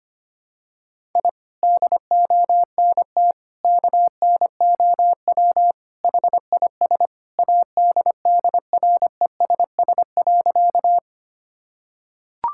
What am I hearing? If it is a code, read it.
Morse code, heard as a keyed tone, about 25 wpm, 700 Hz. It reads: I DONT KNOW HIS ADDRESS.
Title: Sentences from Top 1000 Words - Review Part 6 - 25wpm